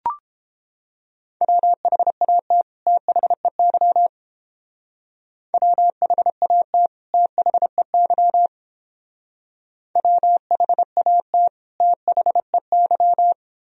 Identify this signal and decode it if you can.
Morse code, heard as a keyed tone, about 29 wpm, 700 Hz. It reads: WHAT THEY WHAT THEY WHAT THEY